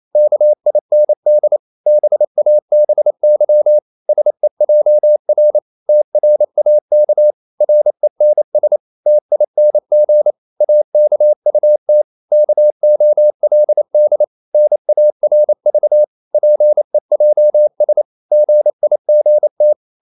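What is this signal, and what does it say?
Morse code, heard as a keyed tone, about 28 wpm, 600 Hz. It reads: KIND BABY SEJR TRAK RENS TING AKUT KOLD NARV PEJS GIGT
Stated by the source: Danish short words sent in Morse at 28 wpm, from CW med Gnister.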